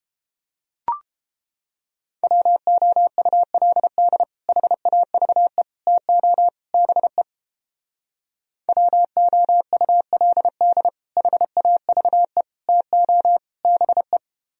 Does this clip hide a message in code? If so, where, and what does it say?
Yes: Morse code from 2.2 to 14.2 s, heard as a keyed tone, about 29 wpm, 700 Hz. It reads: WOULD HAVE TO BE WOULD HAVE TO BE